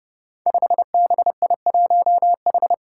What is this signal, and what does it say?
Morse code, heard as a keyed tone, about 30 wpm, 700 Hz. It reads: 5BI1H
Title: Five Challenging Letter-Number Combinations 30wpm